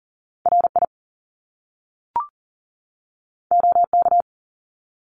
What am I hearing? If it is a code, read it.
Morse code, heard as a keyed tone, about 41 wpm, 700 Hz. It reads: RI OK
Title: Top Two Letter Combinations in Words 40wpm